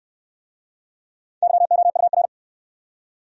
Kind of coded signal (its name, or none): Morse code